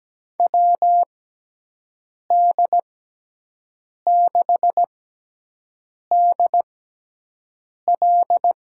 Morse code, heard as a keyed tone, about 17 wpm, spaced as for 6 wpm, 700 Hz. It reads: WD6DL
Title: USA Call Signs - 6 Characters - 6x Character Spacing - 17wpm